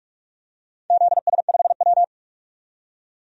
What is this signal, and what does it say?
Morse code, heard as a keyed tone, about 43 wpm, 700 Hz. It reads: ZS5W
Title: DXCC-100 Call Signs - 4 Characters - 45wpm